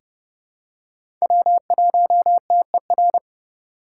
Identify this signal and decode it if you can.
Morse code, heard as a keyed tone, about 31 wpm, 700 Hz. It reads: W1TER
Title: K1USN SST QSO Practice - 30wpm